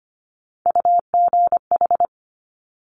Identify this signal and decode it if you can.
Morse code, heard as a keyed tone, about 25 wpm, 700 Hz. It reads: UGH
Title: Top Three Letter Combinations in Words 25wpm